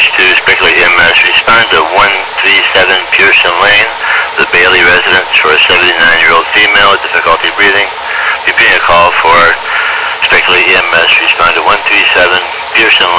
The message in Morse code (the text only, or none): none